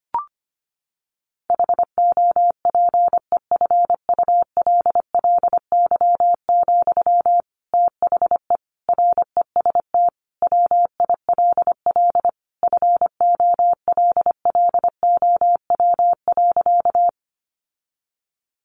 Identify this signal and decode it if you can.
Morse code, heard as a keyed tone, about 25 wpm, 700 Hz. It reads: HOPEFULLY, THE REST WILL FOLLOW.
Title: Sentences from Top 1500 Words 25wpm